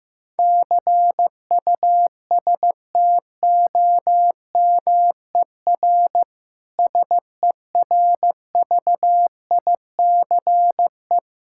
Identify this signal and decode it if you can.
Morse code, heard as a keyed tone, about 16 wpm, 700 Hz. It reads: CUSTOMER SERVICE
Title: Practice QSO - Exchange 3 - More Info - 15wpm